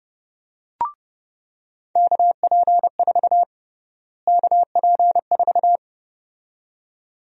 Morse code, heard as a keyed tone, about 30 wpm, 700 Hz. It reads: KP4 KP4
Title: USA Call Sign Prefixes - Repeated 2x - 30wpm